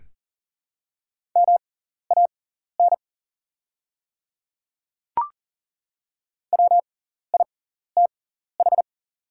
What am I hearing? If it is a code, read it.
Morse code, heard as a keyed tone, about 40 wpm, spaced as for 14 wpm, 700 Hz. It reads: MAN WITH
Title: 6x Character Spacing - Top 300 Words - 40wpm